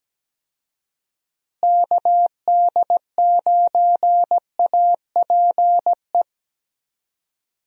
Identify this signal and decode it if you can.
Morse code, heard as a keyed tone, about 17 wpm, 700 Hz. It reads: KD9APE
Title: USA Call Signs - 6 Characters - Repeated 2x - 17wpm